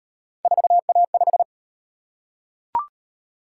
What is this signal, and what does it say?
Morse code, heard as a keyed tone, about 37 wpm, 700 Hz. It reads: 4A5